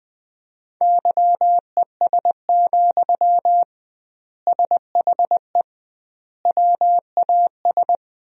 Morse code, heard as a keyed tone, about 21 wpm, 700 Hz. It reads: YES, SHE WAS
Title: 2x Word-Spacing - Sentences from Top 500 Words - 20wpm